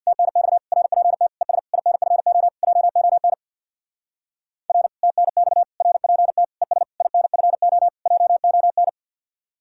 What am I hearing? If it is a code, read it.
Morse code, heard as a keyed tone, about 64 wpm, 700 Hz. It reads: TNX RPT ES INFO JON R TNX RPT ES INFO JON